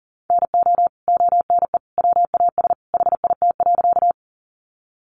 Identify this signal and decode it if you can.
Morse code, heard as a keyed tone, about 38 wpm, 700 Hz. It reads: NO ONE WAS HIT.
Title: Sentences from Top 1200 Words 40wpm